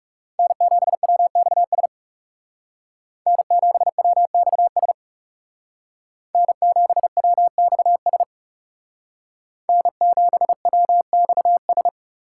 Morse code, heard as a keyed tone, about 37 wpm, 700 Hz. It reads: N7WXS N7WXS N7WXS N7WXS
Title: Speed Racing - USA Call Signs 30wpm